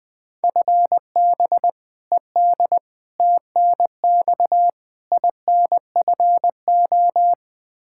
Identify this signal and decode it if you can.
Morse code, heard as a keyed tone, about 20 wpm, 700 Hz. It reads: FB ED TNX INFO
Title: Practice QSO - Exchange 3 - More Info - 20wpm